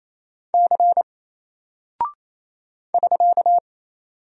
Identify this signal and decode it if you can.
Morse code, heard as a keyed tone, about 28 wpm, 700 Hz. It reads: C <SK>
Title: Single Character (Letter:Number:Punctuation:Prosign) Alt 28wpm